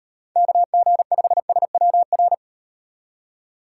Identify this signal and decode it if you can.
Morse code, heard as a keyed tone, about 38 wpm, 700 Hz. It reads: KG5SWR